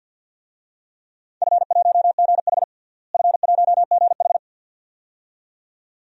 Morse code, heard as a keyed tone, about 48 wpm, 700 Hz. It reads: F1GH F1GH
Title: DXCC-100 Call Signs - 4 Characters - Repeated 2x - 50wpm